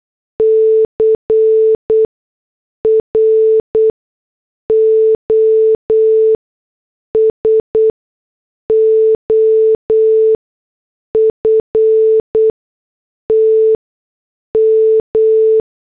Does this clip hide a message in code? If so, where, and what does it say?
Yes: Morse code from 0.4 to 15.6 s, heard as a keyed tone, about 8 wpm, spaced as for 6 wpm, 440 Hz. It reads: CROSOFT M